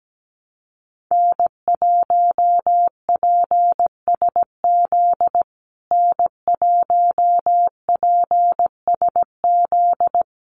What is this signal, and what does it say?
Morse code, heard as a keyed tone, about 17 wpm, 700 Hz. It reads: N1PSZ N1PSZ
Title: USA Call Signs Repeated 3x 17wpm